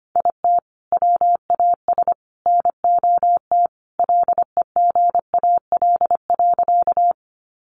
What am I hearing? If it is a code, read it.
Morse code, heard as a keyed tone, about 25 wpm, 700 Hz. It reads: IT WAS NOT LEGAL.
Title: Sentences from Top 1200 Words 25wpm